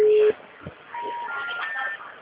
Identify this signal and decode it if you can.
Special information tone, with speech.
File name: out- -2020- - - .34441